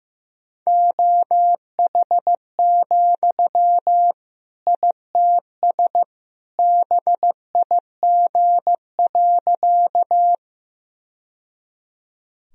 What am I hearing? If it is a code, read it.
Morse code, heard as a keyed tone, about 15 wpm, 700 Hz. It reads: OH, ITS BIG.
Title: Sentences from Top 1000 Words - Review Part 1 - 15wpm